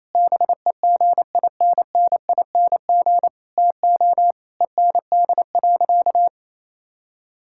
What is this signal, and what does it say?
Morse code, heard as a keyed tone, about 27 wpm, 700 Hz. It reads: BEGINNING TO END.
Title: Sentences from Top 1000 Words - Review Part 5 - 28wpm